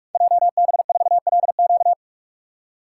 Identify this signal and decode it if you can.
Morse code, heard as a keyed tone, about 44 wpm, 700 Hz. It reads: JB4LQ
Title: Code Groups - 5 Letter:Numbers - 45wpm